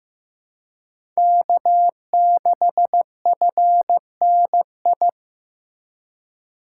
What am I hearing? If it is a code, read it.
Morse code, heard as a keyed tone, about 15 wpm, 700 Hz. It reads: K6FNI
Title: USA Call Signs - 3x Character Spacing - 15wpm